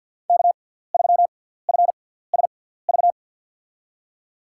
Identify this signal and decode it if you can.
Morse code, heard as a keyed tone, about 48 wpm, spaced as for 17 wpm, 700 Hz. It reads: K3FSV
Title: USA Call Signs - 5 Characters - 6x Character Spacing - 50wpm